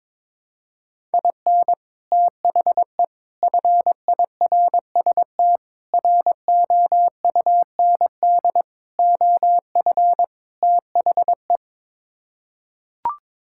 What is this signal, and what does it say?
Morse code, heard as a keyed tone, about 22 wpm, 700 Hz. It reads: IN THE FIRST ROUND OF THE
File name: practice / 22wpm / Sets of 6 Words - 22wpm